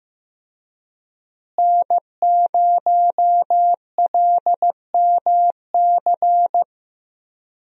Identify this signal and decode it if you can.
Morse code, heard as a keyed tone, about 15 wpm, 700 Hz. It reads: N0LMC
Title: USA Call Signs - 5 Characters - Repeated 3x - 15wpm